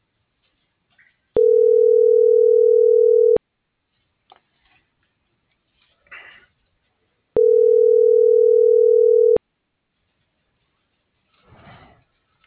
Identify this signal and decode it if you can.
Ringback tone (North America), 440 + 480 Hz.